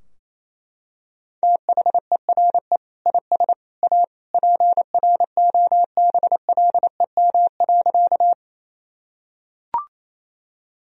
Morse code, heard as a keyed tone, about 28 wpm, 700 Hz. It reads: THERE IS A PROBLEM.
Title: Sentences from Top 1000 Words - Review Part 2 - 28wpm